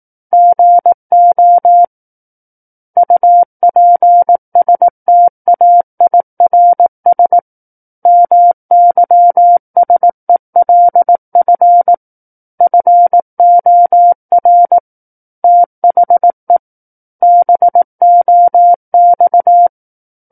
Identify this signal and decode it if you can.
Morse code, heard as a keyed tone, about 18 wpm, 700 Hz. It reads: GO UPSTAIRS MYSELF FOR THE BOX
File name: 18 WPM